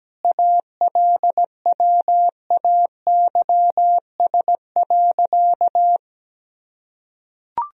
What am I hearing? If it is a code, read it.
Morse code, heard as a keyed tone, about 17 wpm, 700 Hz. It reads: ALWAYS.
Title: Sentences from Top 1000 Words - Review Part 4 - 17wpm